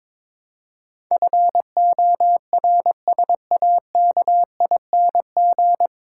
Morse code, heard as a keyed tone, about 22 wpm, 700 Hz. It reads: FORSAKING
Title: Pangram Challenge v2 - 22wpm